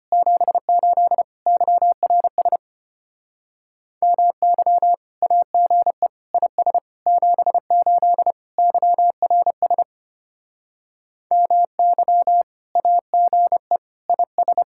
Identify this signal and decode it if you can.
Morse code, heard as a keyed tone, about 33 wpm, 700 Hz. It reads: 78 YRS MY AGE IS 78 YRS MY AGE IS